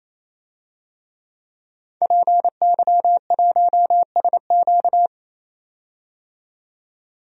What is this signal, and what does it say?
Morse code, heard as a keyed tone, about 27 wpm, 700 Hz. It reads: PY1SQ
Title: DXCC-100 Call Signs - 5 Characters - 28wpm